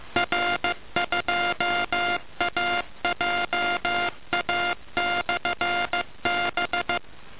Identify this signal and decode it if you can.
Morse code, heard as a keyed tone, about 15 wpm, 700 Hz, with noise. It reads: R2AJA/B
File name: uploader / fs1000-am-bcn